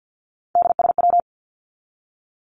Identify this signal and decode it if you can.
Morse code, heard as a keyed tone, about 50 wpm, 700 Hz. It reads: DSW
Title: Abbreviations 50wpm